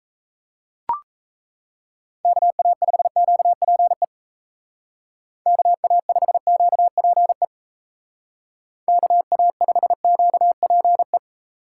Morse code, heard as a keyed tone, about 34 wpm, 700 Hz. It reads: KA5QPE KA5QPE KA5QPE
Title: Speed Racing - USA 2x3 Call Signs 28wpm